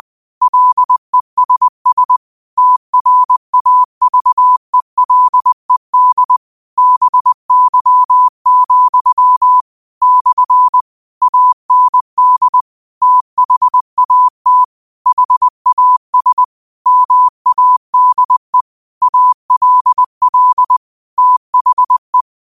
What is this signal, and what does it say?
Morse code, heard as a keyed tone, about 20 wpm, 1000 Hz. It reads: LESS TRAVELED BY, / AND THAT HAS MADE ALL THE
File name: qotdfiles / 20wpm